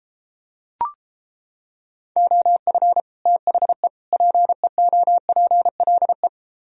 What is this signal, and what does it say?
Morse code, heard as a keyed tone, about 29 wpm, 700 Hz. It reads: OF THE PEOPLE